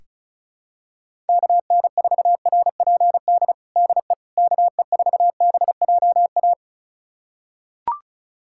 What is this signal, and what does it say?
Morse code, heard as a keyed tone, about 34 wpm, 700 Hz. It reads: KN4RPD DE KE4BJA
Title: USA Callsign DE Callsign - 6 Characters - 35wpm